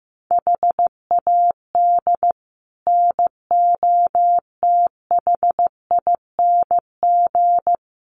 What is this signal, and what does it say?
Morse code, heard as a keyed tone, about 14 wpm, 700 Hz. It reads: HAD NOTHING
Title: Sets of 5 Words 15wpm